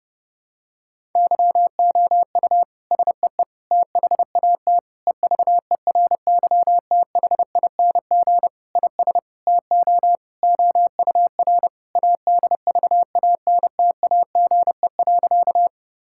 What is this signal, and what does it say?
Morse code, heard as a keyed tone, about 30 wpm, 700 Hz. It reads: YOU SEE THAT EVERYTHING IS TO OUR ADVANTAGE.
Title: Sentences from Top 1000 Words - Review Part 6 - 30wpm